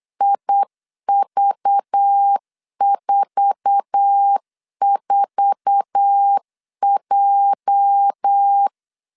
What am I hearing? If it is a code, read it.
Morse code, heard as a keyed tone, about 8 wpm, 800 Hz. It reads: IV44J
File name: LEZIONI / 8 YJ45 / 8 YJ4 8wpm 3246